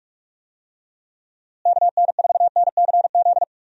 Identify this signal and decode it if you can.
Morse code, heard as a keyed tone, about 46 wpm, 700 Hz. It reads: KN4NCZ
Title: USA Call Signs - 6 Characters - Repeated 3x - 45wpm